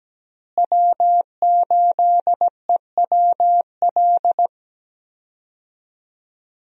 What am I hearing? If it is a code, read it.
Morse code, heard as a keyed tone, about 17 wpm, 700 Hz. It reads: W8EWL
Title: USA Call Signs - 5 Characters - Repeated 3x - 17wpm